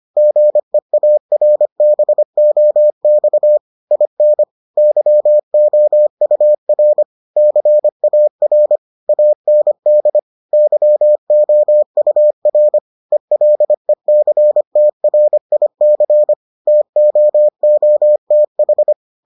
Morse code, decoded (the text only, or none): GEARBOX IN YOUR CAR AND YOUR ELECTRIC TOOTH